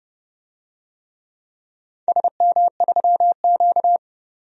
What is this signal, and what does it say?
Morse code, heard as a keyed tone, about 30 wpm, 700 Hz. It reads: SM3Q
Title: DXCC-100 Call Signs - 4 Characters - 30wpm